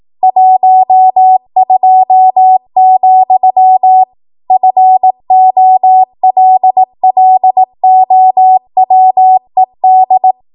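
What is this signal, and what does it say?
Morse code, heard as a keyed tone, about 18 wpm, 750 Hz. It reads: 12, FOLLOWED
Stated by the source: ARRL 18 wpm Morse code practice archive transmission.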